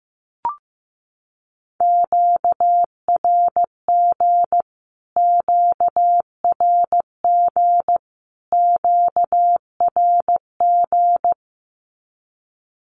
Morse code, heard as a keyed tone, about 14 wpm, 700 Hz. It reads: QRG QRG QRG
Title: Q-Codes Repeated 3x 15wpm